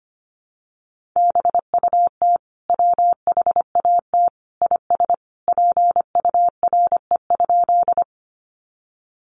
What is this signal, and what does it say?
Morse code, heard as a keyed tone, about 25 wpm, 700 Hz. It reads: BUT WHAT IS PURE?
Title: Sentences from Top 1500 Words 25wpm